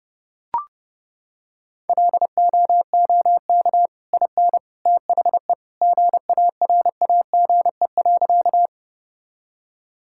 Morse code, heard as a keyed tone, about 32 wpm, 700 Hz. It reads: LOOK IN THE GARAGE.
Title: Sentences from Top 1600 Words 30wpm